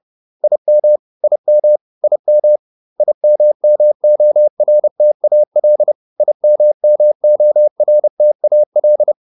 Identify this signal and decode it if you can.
Morse code, heard as a keyed tone, about 30 wpm, 600 Hz. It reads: IM IM IM IMMORTAL IMMORTAL
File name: IM8 30